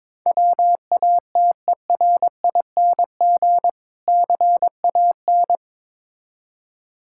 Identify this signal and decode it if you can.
Morse code, heard as a keyed tone, about 22 wpm, 690 Hz. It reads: WATERING CAN
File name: Difficult Words with Hints - 22wpm